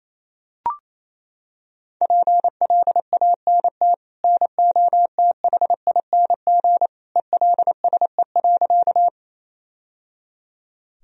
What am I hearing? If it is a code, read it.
Morse code, heard as a keyed tone, about 28 wpm, 700 Hz. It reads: PLANT NOTHING ELSE.
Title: Sentences from Top 1000 Words - Review Part 1 - 28wpm